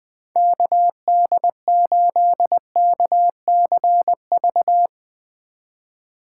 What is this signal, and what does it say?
Morse code, heard as a keyed tone, about 20 wpm, 700 Hz. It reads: KD8KCV
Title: USA Call Signs - 2x Character Spacing - 20wpm